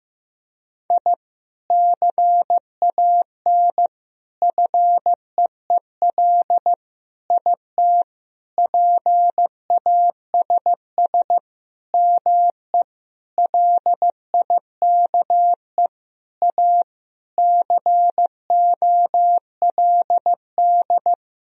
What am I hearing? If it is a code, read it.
Morse code, heard as a keyed tone, about 16 wpm, 700 Hz. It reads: I CAN FEEL IT PASS ME LIKE A COLD
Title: Sentences from Top 1000 Words - Review Part 2 - 15wpm